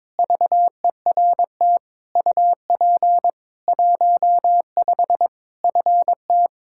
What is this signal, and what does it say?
Morse code, heard as a keyed tone, about 21 wpm, 700 Hz. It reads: VERT UP 15 FT
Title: Practice QSO - Exchange 2 - Info - 22wpm